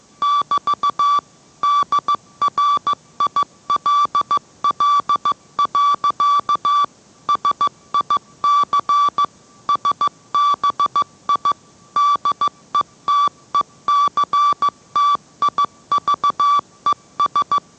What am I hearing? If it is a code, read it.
Morse code, heard as a keyed tone, about 15 wpm, 1160 Hz. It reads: <BT> DRILL. SIC SBI DETECTIVES